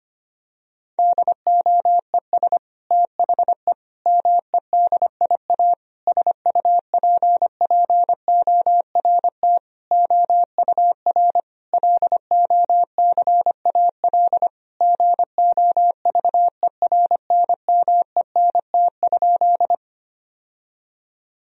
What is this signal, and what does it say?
Morse code, heard as a keyed tone, about 25 wpm, 700 Hz. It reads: DOES THE MEDIA SUPPORT OUR LOCAL GOVERNMENT?
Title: Sentences from Top 1000 Words - Review Part 4 - 25wpm